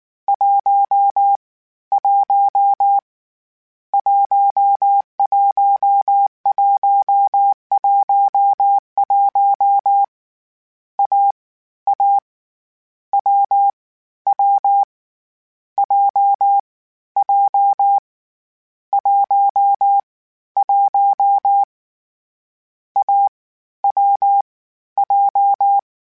Morse code, heard as a keyed tone, about 19 wpm, 800 Hz. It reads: 1 1 11111 A A W W J J 1 1 A W J
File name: Difficult Combos / Difficult Combos-mp3s / AWJ1 @18